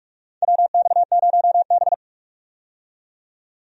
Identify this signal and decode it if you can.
Morse code, heard as a keyed tone, about 45 wpm, 700 Hz. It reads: WX0B
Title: USA Call Signs - 4 Characters - 45wpm